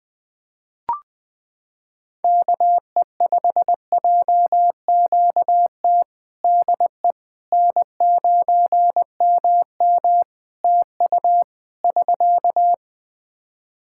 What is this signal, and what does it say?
Morse code, heard as a keyed tone, about 21 wpm, 700 Hz. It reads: KE5JQT DE N9MM TU <SK>
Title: K1USN SST QSO Practice 20wpm